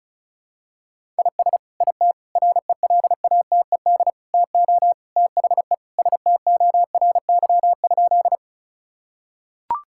Morse code, heard as a keyed tone, about 34 wpm, 700 Hz. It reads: IS IT RELATED TO THE STORY?
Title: Sentences from Top 1000 Words - Review Part 5 - 35wpm